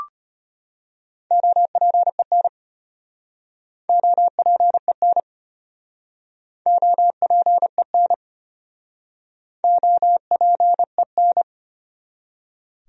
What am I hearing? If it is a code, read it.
Morse code, heard as a keyed tone, about 34 wpm, 700 Hz. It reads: OPEN OPEN OPEN OPEN